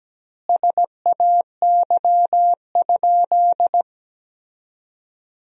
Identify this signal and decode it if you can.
Morse code, heard as a keyed tone, about 17 wpm, 700 Hz. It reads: SAY?
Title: 2x Word-Spacing - Sentences from Top 200 Words - 17wpm